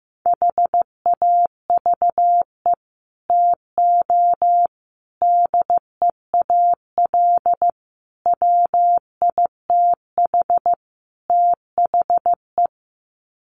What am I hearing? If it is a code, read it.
Morse code, heard as a keyed tone, about 15 wpm, 700 Hz. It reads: HAVE TO DEAL WITH THE
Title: Sets of 5 Words Encore2 15wpm